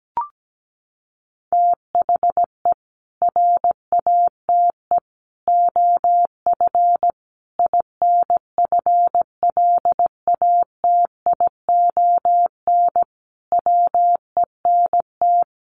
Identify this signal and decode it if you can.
Morse code, heard as a keyed tone, about 17 wpm, 700 Hz. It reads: THE RATE OF INFLATION WENT
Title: Sentences from Top 1100 Words 17wpm